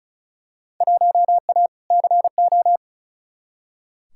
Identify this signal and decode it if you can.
Morse code, heard as a keyed tone, about 36 wpm, 700 Hz. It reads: 1A CO
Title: Field Day Exchanges - 35wpm